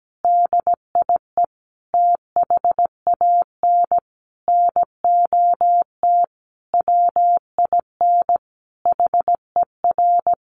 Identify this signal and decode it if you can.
Morse code, heard as a keyed tone, about 17 wpm, 700 Hz. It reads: DIE THAN NOT WIN HER
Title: Sentences from Top 1500 Words 17wpm